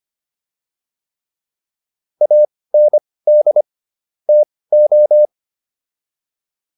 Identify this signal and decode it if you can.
Morse code, heard as a keyed tone, about 25 wpm, spaced as for 18 wpm, 600 Hz. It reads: AND TO
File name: WD104 18